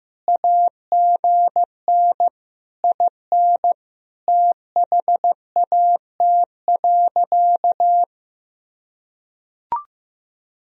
Morse code, heard as a keyed tone, about 15 wpm, 700 Hz. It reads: AGN IN THAT.
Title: Sentences from Top 200 Words and Abbreviations 15wpm